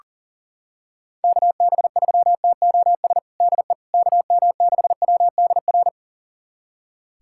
Morse code, heard as a keyed tone, about 40 wpm, 700 Hz. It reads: KB3TOS DE KM6WDR
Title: USA Callsign DE Callsign - 6 Characters - 40wpm